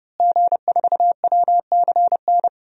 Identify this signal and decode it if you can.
Morse code, heard as a keyed tone, about 30 wpm, 700 Hz. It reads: G4WCN
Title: DXCC-100 Call Signs - 5 Characters - Repeated 2x - 30wpm